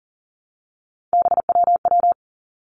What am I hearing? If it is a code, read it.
Morse code, heard as a keyed tone, about 40 wpm, 700 Hz. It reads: BWW
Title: Three Challenging Letter-Number Combinations 40wpm